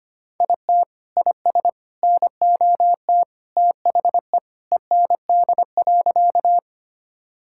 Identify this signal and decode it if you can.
Morse code, heard as a keyed tone, about 25 wpm, 700 Hz. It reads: IT IS NOT THE END.
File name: practice / 25wpm / Sentences with 3-Word Phrases - 25wpm